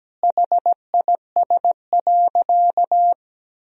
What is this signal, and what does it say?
Morse code, heard as a keyed tone, about 17 wpm, 700 Hz. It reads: HIS.
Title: Sentences from Top 1000 Words - Review Part 3 - 17wpm